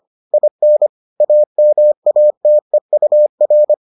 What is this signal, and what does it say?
Morse code, heard as a keyed tone, about 25 wpm, 600 Hz. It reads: IN AMATEUR